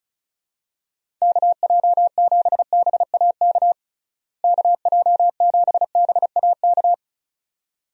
Morse code, heard as a keyed tone, about 34 wpm, 700 Hz. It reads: KJ7BAK KJ7BAK
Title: USA Call Signs - 6 Characters - Repeated 2x - 35wpm